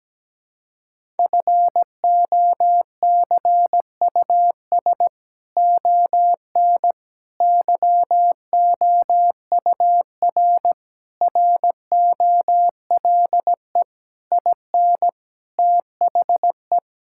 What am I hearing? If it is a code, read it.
Morse code, heard as a keyed tone, about 17 wpm, 700 Hz. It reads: FOCUS ON YOUR ROLE IN THE
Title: Sentences from Top 1000 Words - Review Part 6 - 17wpm